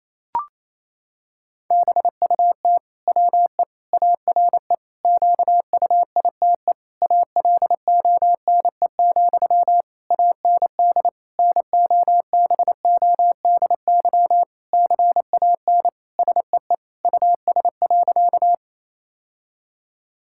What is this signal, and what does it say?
Morse code, heard as a keyed tone, about 28 wpm, 700 Hz. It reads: BUT WE ARE QUITE ALONE, AND NOBODY CAN SEE US.
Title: Sentences from Top 1600 Words 28wpm